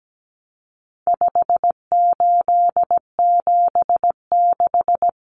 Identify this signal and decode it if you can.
Morse code, heard as a keyed tone, about 17 wpm, 700 Hz. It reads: 5876
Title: Numbers - 4-Digit 17wpm